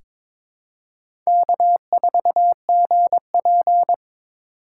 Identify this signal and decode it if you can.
Morse code, heard as a keyed tone, about 22 wpm, 700 Hz. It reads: K4GP
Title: USA Call Signs - 4 Characters - 2x Character Spacing - 22wpm